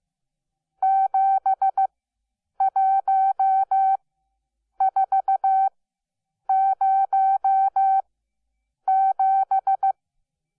Morse code, heard as a keyed tone, about 15 wpm, spaced as for 8 wpm, 770 Hz. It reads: 71407
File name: Onlinelehrgang / morsen / DJ4UF Morsen12